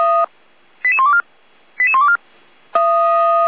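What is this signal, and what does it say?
Five-tone selective call (ZVEI), heard as quick runs of five tones, tones of about 80 ms. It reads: 88124 88124
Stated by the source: ZVEI five-tone selective call received by SDR.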